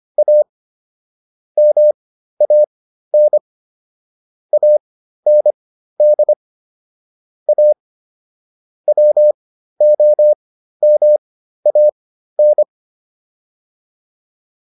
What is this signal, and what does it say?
Morse code, heard as a keyed tone, about 25 wpm, spaced as for 13 wpm, 600 Hz. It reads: A MAN AND A WOMAN